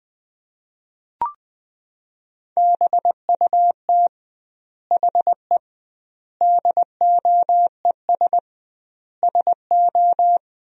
Morse code, heard as a keyed tone, about 21 wpm, 700 Hz. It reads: BUT HE DOES SO
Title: 2x Word-Spacing - Sentences from Top 200 Words - 20wpm